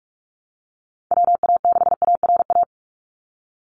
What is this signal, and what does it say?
Morse code, heard as a keyed tone, about 46 wpm, 700 Hz. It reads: WA6ARA
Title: NA SOTA Chaser Callsigns 45wpm